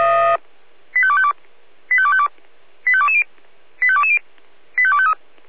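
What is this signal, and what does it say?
Five-tone selective call (ZVEI), heard as quick runs of five tones, tones of about 70 ms. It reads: 85252 85252 85220 85220 85253
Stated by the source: ZVEI five-tone call received on an SDR, channel 496.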